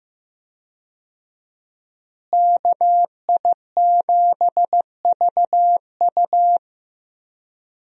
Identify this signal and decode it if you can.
Morse code, heard as a keyed tone, about 15 wpm, 700 Hz. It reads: KI7VU
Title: USA Call Signs - 5 Characters - Repeated 2x - 15wpm